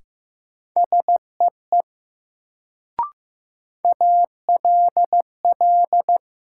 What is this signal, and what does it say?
Morse code, heard as a keyed tone, about 15 wpm, 700 Hz. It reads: SEE ALL
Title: Sentences from Top 100 Words - New Word Drill - 15wpm